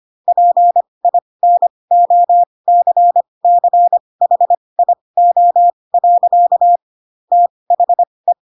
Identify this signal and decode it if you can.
Morse code, heard as a keyed tone, about 25 wpm, spaced as for 20 wpm, 700 Hz. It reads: PINOCCHIO. THE
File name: Pinocchio / Pinocchio ch3 15 25 ik1hge